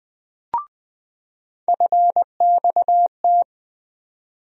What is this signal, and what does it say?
Morse code, heard as a keyed tone, about 20 wpm, 700 Hz. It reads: FXT
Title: USA Call Sign Postfixes - 3 Letters 20wpm